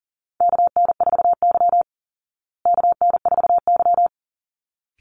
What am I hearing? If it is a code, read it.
Morse code, heard as a keyed tone, about 40 wpm, 700 Hz. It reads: KN4Y KN4Y
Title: Call Signs Repeated 3x - 40wpm